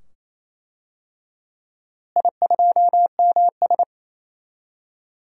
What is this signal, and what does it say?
Morse code, heard as a keyed tone, about 28 wpm, 700 Hz. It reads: I2MS
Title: DXCC-100 Call Signs - 4 Characters - Repeated 2x - 28wpm